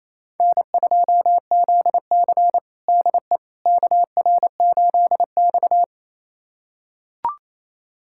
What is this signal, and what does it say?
Morse code, heard as a keyed tone, about 27 wpm, 700 Hz. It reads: N2ZC DE KR8X
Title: USA Callsign DE Callsign - 4 Characters - 28wpm